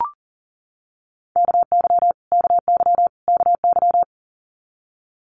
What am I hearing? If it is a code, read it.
Morse code, heard as a keyed tone, about 37 wpm, 700 Hz. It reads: KY KY KY
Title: Multi-letter Phonemes Repeated 3x 40wpm